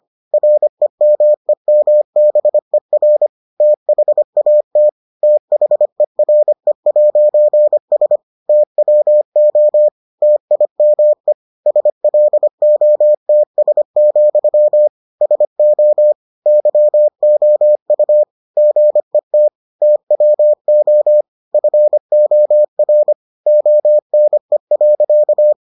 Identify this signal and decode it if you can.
Morse code, heard as a keyed tone, about 25 wpm, 600 Hz. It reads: REMEMBER THAT THERE'S TWO TIME SLOTS, SO YOU GET TWO FOR ONE.